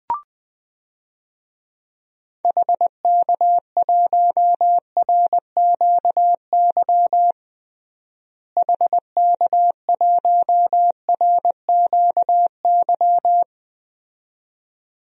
Morse code, heard as a keyed tone, about 21 wpm, 700 Hz. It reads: HK1RQY HK1RQY
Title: DXCC-100 Call Signs - 6 Characters - Repeated 2x - 20wpm